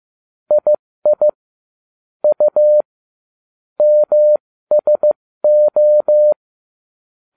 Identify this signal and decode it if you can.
Morse code, heard as a keyed tone, about 15 wpm, spaced as for 13 wpm, 600 Hz. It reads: II U MSO
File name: cw / ubacw20 3